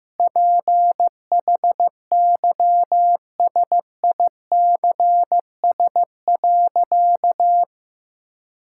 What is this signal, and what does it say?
Morse code, heard as a keyed tone, about 16 wpm, 700 Hz. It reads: PHYSICS.
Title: Sentences from Top 1000 Words - Review Part 5 - 15wpm